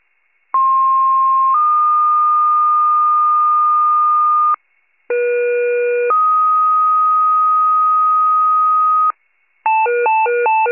Two-tone page, heard as a one-second tone, then a three-second tone, about 1060 then 1220 Hz, about 480 then 1250 Hz. It is heard from 0.5 to 9.1 s.
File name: wcalert